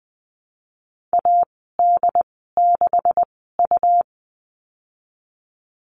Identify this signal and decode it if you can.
Morse code, heard as a keyed tone, about 20 wpm, spaced as for 15 wpm, 700 Hz. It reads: AD6U
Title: 140 Lesson 23g - Letter f - Call Signs 20wpm